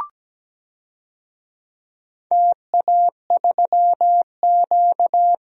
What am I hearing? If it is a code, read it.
Morse code, heard as a keyed tone, about 17 wpm, 700 Hz. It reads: TA3Q
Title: DXCC-100 Call Signs - 4 Characters - Repeated 3x - 17wpm